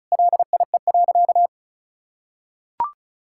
Morse code, heard as a keyed tone, about 36 wpm, 700 Hz. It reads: LIE.
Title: Sentences from Top 1600 Words 35wpm